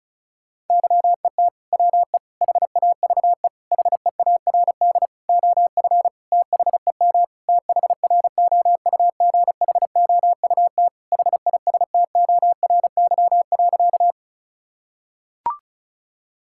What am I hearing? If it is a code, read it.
Morse code, heard as a keyed tone, about 34 wpm, 700 Hz. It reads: YET WE HAVE HEARD OF THEM THROUGHOUT HISTORY.
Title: Sentences from Top 1000 Words - Review Part 5 - 35wpm